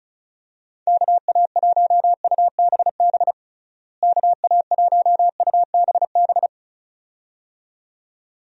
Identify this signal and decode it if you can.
Morse code, heard as a keyed tone, about 34 wpm, 700 Hz. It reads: KA1UBB KA1UBB
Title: USA Call Signs - 6 Characters - Repeated 2x - 35wpm